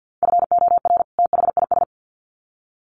Morse code, heard as a keyed tone, about 50 wpm, 700 Hz. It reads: FOR THIS
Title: Sets of 2 Words 50wpm